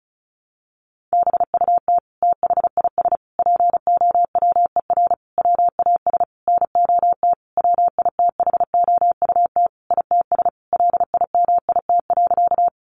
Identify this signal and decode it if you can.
Morse code, heard as a keyed tone, about 34 wpm, 700 Hz. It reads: BUT THIS POWER WAS NOT WITHOUT ITS LIMIT.
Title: Sentences from Top 1500 Words 35wpm